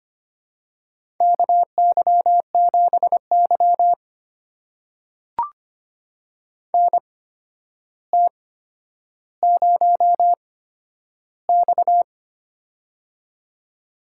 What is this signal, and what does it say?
Morse code, heard as a keyed tone, about 25 wpm, spaced as for 11 wpm, 700 Hz. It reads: KY7Y NT0X
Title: USA Call Signs - 8x Character Spacing - 25wpm